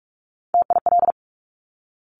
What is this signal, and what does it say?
Morse code, heard as a keyed tone, about 46 wpm, 700 Hz. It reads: TIL
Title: Top Three Letter Combinations in Words 45wpm